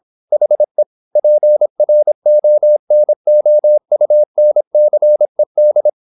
Morse code, heard as a keyed tone, about 26 wpm, 600 Hz. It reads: HE PRONOUNCED
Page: Never 26wpm | CW med Gnister